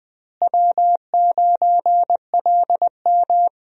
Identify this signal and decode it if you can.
Morse code, heard as a keyed tone, about 20 wpm, 700 Hz. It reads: W9LM
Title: USA Call Signs - 4 Characters - Repeated 2x - 20wpm